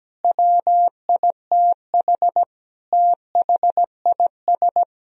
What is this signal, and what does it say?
Morse code, heard as a keyed tone, about 17 wpm, 700 Hz. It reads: WITH THIS
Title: Sentences from Top 1000 Words - Review Part 2 - 17wpm